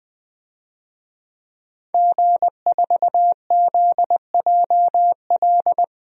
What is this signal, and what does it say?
Morse code, heard as a keyed tone, about 20 wpm, 700 Hz. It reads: G4ZJL
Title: DXCC-100 Call Signs - 5 Characters - 20wpm